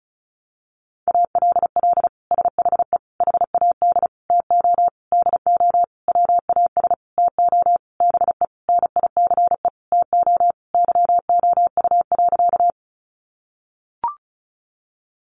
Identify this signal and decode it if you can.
Morse code, heard as a keyed tone, about 34 wpm, 700 Hz. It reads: ALL SHE HAD TO DO WAS TO BE NICE TO YOU.
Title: Sentences from Top 1100 Words 35wpm